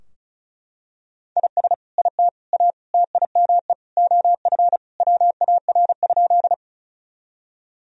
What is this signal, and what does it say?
Morse code, heard as a keyed tone, about 36 wpm, 700 Hz. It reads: IS IT A TIME OF WAR?
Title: Sentences from Top 1000 Words - Review Part 2 - 35wpm